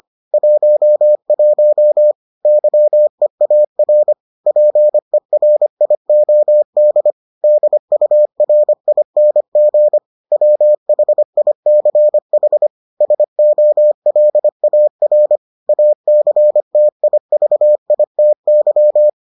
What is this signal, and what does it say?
Morse code, heard as a keyed tone, about 25 wpm, 600 Hz. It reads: 11 YEAR PERIOD DURING WHICH SOLAR ACTIVITY